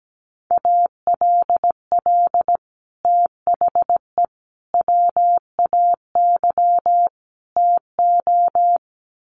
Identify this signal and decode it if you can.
Morse code, heard as a keyed tone, about 17 wpm, 700 Hz. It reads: ALL THE WAY TO
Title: Sets of 4 Words 17wpm